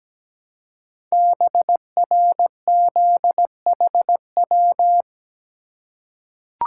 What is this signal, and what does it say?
Morse code, heard as a keyed tone, about 17 wpm, 700 Hz. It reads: BRZHW
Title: Code Groups - 5 Letters - 17wpm